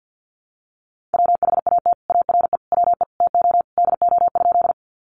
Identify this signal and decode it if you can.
Morse code, heard as a keyed tone, about 50 wpm, 700 Hz. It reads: WHAT ARE WE TO DO?
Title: Sentences from Top 100 Words 50wpm